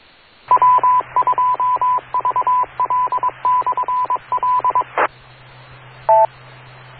Morse code, heard as a keyed tone, about 22 wpm, 1000 Hz, with radio noise. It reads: W2VL/L